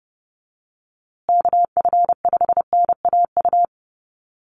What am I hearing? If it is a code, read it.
Morse code, heard as a keyed tone, about 30 wpm, 700 Hz. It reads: KF5NAU
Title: USA Call Signs Repeated 2x 30wpm